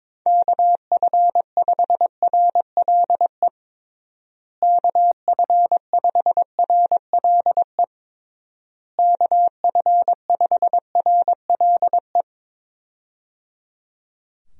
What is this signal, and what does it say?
Morse code, heard as a keyed tone, about 22 wpm, 700 Hz. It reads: KF5RLE KF5RLE KF5RLE
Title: USA Call Signs - 6 Characters - Repeated 3x - 22wpm